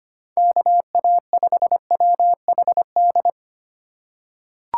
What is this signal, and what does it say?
Morse code, heard as a keyed tone, about 25 wpm, 700 Hz. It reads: KA5WHD
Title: USA Call Signs - 6 Characters - 3x Character Spacing - 25wpm